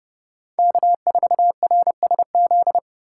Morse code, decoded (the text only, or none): K4RSZ